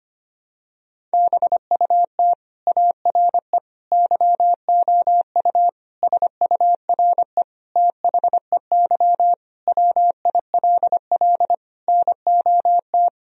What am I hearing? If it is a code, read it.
Morse code, heard as a keyed tone, about 25 wpm, 700 Hz. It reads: BUT ARE YOU SURE THEY WILL NOT
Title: Sentences from Top 1000 Words - Review Part 1 - 25wpm